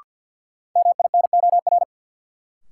Morse code, heard as a keyed tone, about 53 wpm, 700 Hz. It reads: MINOR
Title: Sentences from Top 1500 Words - New Word Drill - 50wpm